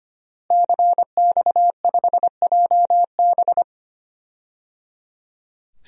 Morse code, heard as a keyed tone, about 25 wpm, 700 Hz. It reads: CX5JB
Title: DXCC-100 Call Signs - 5 Characters - Repeated 3x - 25wpm